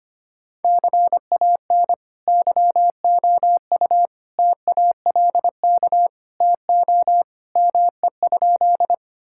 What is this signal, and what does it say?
Morse code, heard as a keyed tone, about 25 wpm, 700 Hz. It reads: CAN YOU TALK TO ME?
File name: practice / 25wpm / Sentences with 2-Word Phrases - 25wpm